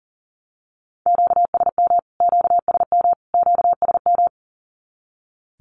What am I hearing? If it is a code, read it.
Morse code, heard as a keyed tone, about 40 wpm, 700 Hz. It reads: QSM QSM QSM
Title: Q-Codes Repeated 3x 40wpm